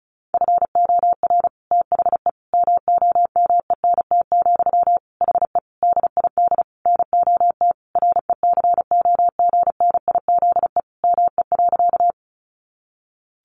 Morse code, heard as a keyed tone, about 34 wpm, 700 Hz. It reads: FOR THE MOMENT, HE DID NOT RECOGNIZE ME.
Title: Sentences from Top 1400 Words 35wpm